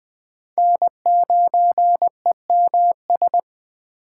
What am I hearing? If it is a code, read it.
Morse code, heard as a keyed tone, about 21 wpm, 700 Hz. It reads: N9EMS
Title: USA Call Signs - 5 Characters - 4x Character Spacing - 20wpm